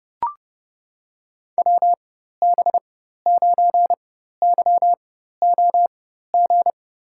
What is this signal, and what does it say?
Morse code, heard as a keyed tone, about 32 wpm, spaced as for 14 wpm, 700 Hz. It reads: WB9YOG
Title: USA Call Signs - 6 Characters - 4x Character Spacing - 30wpm